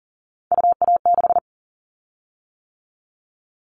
Morse code, heard as a keyed tone, about 38 wpm, 700 Hz. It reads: UA6
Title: Callsign Prefixes for Top 50 DXCC - 40wpm